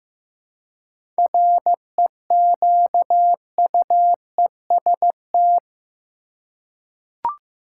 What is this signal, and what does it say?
Morse code, heard as a keyed tone, about 15 wpm, 700 Hz. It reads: REQUEST
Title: Seven Letter Words - 15wpm